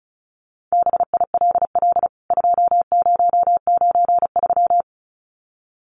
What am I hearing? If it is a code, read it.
Morse code, heard as a keyed tone, about 34 wpm, 700 Hz. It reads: BILL 2093
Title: CWT Exchange 35wpm